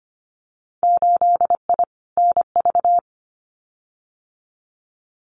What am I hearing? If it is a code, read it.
Morse code, heard as a keyed tone, about 25 wpm, 700 Hz. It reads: 8I NV
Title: Winter Field Day Exchanges Encore 25wpm